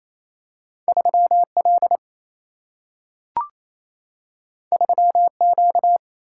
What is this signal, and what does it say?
Morse code, heard as a keyed tone, about 28 wpm, 700 Hz. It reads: 3L 3Q
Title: Single-Letter Single-Number 28wpm